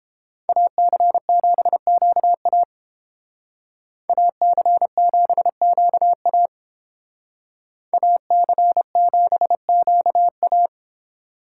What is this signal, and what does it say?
Morse code, heard as a keyed tone, about 28 wpm, 700 Hz. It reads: AC7QA AC7QA AC7QA